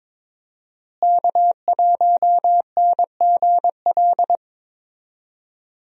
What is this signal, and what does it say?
Morse code, heard as a keyed tone, about 21 wpm, 700 Hz. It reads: K1NGL
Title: USA Call Signs - 5 Characters - Repeated 3x - 22wpm